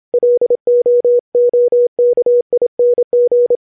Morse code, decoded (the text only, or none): LOOKING